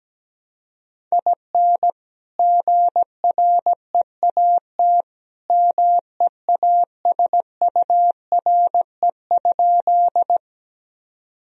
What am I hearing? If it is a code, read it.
Morse code, heard as a keyed tone, about 17 wpm, 700 Hz. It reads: IN GREAT MEASURE?
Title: Sentences from Top 1000 Words - Review Part 2 - 17wpm